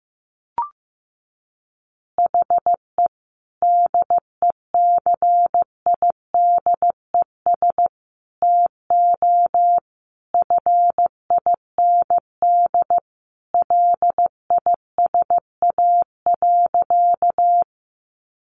Morse code, heard as a keyed tone, about 15 wpm, 700 Hz. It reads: HE DECIDES TO FIND LISA.
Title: Additional Easy Phrases 15wpm